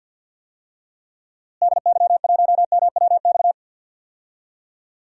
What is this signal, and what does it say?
Morse code, heard as a keyed tone, about 52 wpm, 700 Hz. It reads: DY1MWX
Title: DXCC-100 Call Signs - 6 Characters - Repeated 3x - 50wpm